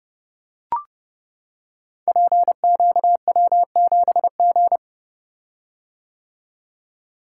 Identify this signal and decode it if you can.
Morse code, heard as a keyed tone, about 30 wpm, 700 Hz. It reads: PQW7G
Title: Code Groups - 5 Letter:Numbers - 30wpm